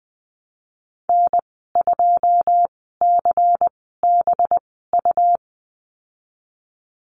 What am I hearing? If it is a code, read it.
Morse code, heard as a keyed tone, about 20 wpm, 700 Hz. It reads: N 2 C B U
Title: 231 Lesson 37g - Letter x - Call Signs 20wpm